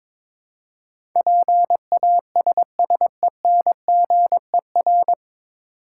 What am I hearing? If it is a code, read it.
Morse code, heard as a keyed tone, about 22 wpm, 690 Hz. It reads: PASSENGER
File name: 9 Letter Words - First 3 Letters Given - 22wpm